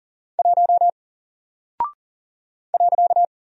Morse code, heard as a keyed tone, about 40 wpm, 700 Hz. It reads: 1 .
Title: Single Character (Letter:Number:Punctuation:Prosign) 40wpm